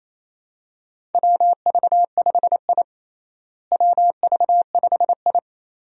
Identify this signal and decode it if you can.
Morse code, heard as a keyed tone, about 28 wpm, 700 Hz. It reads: WV5I WV5I
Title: USA Call Signs - 4 Characters - Repeated 2x - 28wpm